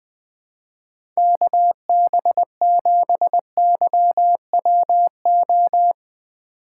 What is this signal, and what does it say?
Morse code, heard as a keyed tone, about 20 wpm, 700 Hz. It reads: KB7YWO
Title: USA Call Signs - 2x Character Spacing - 20wpm